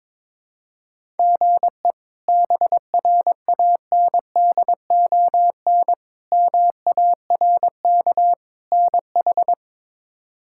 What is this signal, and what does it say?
Morse code, heard as a keyed tone, about 21 wpm, 700 Hz. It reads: GE BRANDON MARK NH
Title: K1USN SST QSO Practice - 22wpm